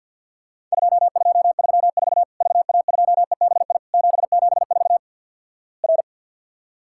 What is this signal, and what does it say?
Morse code, heard as a keyed tone, about 50 wpm, 700 Hz. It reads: 2234 VA2EBI 774 R
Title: Worked All Europe DX Contest - QTC Exchange - 50wpm